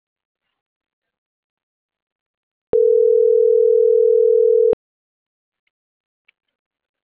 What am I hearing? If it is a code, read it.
Ringback tone (North America), 440 + 480 Hz.